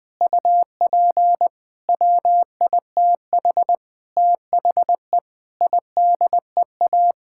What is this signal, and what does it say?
Morse code, heard as a keyed tone, about 20 wpm, 700 Hz. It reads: UP WITH THE IDEA
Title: Sets of 8 Words - 20wpm